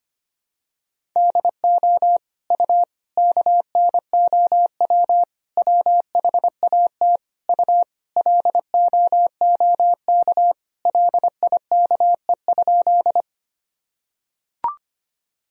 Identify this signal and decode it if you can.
Morse code, heard as a keyed tone, about 25 wpm, 700 Hz. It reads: DO U KNOW WHAT U LOOK LIKE?
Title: Sentences from Top 100 Words and Abbreviations 25wpm